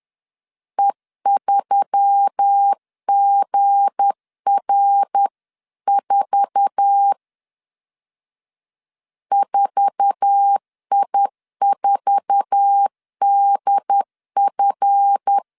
Morse code, heard as a keyed tone, about 10 wpm, 800 Hz. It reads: E3GR4 4I4DF
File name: LEZIONI / 8 YJ45 / 8 YJ4 10wpm 2633